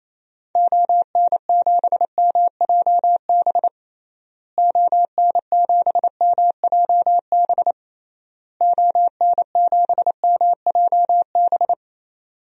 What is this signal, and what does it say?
Morse code, heard as a keyed tone, about 27 wpm, 700 Hz. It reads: ON7MJB ON7MJB ON7MJB
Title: DXCC-100 Call Signs - 6 Characters - Repeated 3x - 28wpm